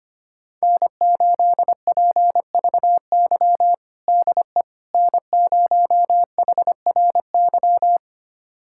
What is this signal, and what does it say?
Morse code, heard as a keyed tone, about 25 wpm, 700 Hz. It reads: N8PVY DE N0HRY
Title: USA Callsign DE Callsign - 5 Characters - 25wpm